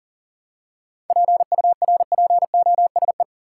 Morse code, heard as a keyed tone, about 40 wpm, 700 Hz. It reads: PURPOSE
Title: Seven Letter Words - 40wpm